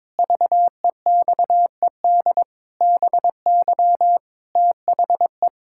Morse code, heard as a keyed tone, about 22 wpm, 700 Hz. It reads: VEXED BY THE
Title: Pangram Challenge v2 - 22wpm